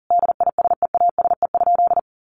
Morse code, heard as a keyed tone, about 40 wpm, 700 Hz. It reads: DISEASE?
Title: Sentences from Top 1500 Words 40wpm